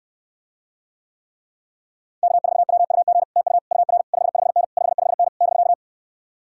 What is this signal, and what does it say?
Morse code, heard as a keyed tone, about 68 wpm, 700 Hz. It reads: K4WUG TU UR 55N 55N <BK>